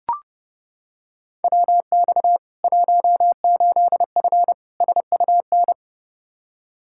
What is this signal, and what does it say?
Morse code, heard as a keyed tone, about 30 wpm, 700 Hz. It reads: WX 18F SUN
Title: Practice QSO - Exchange 2 - Info - 30wpm